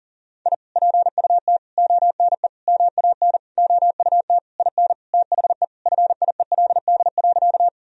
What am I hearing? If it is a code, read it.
Morse code, heard as a keyed tone, about 40 wpm, 700 Hz. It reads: I PUT ONE MAN OUT IN THE FIELD.
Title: Sentences from Top 1000 Words - Review Part 2 - 40wpm